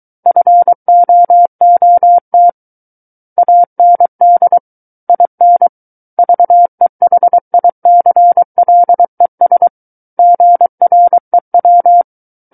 Morse code, heard as a keyed tone, about 23 wpm, 700 Hz. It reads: FOOT AND IN VEHICLES GREW
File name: Book One / 14-Chapter 14 - 23 WPM